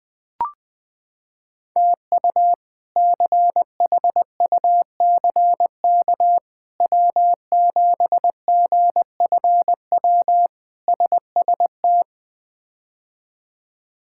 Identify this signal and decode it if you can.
Morse code, heard as a keyed tone, about 20 wpm, 700 Hz. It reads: TU CHUCK W7GFW SST
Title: K1USN SST QSO Practice - 20wpm